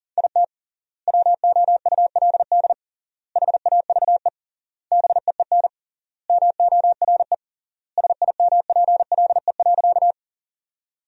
Morse code, heard as a keyed tone, about 40 wpm, 700 Hz. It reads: IT WOULD HAVE BEEN MORE SIMPLE.
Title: 3x Word-Spacing - Sentences from Top 400 Words - 40wpm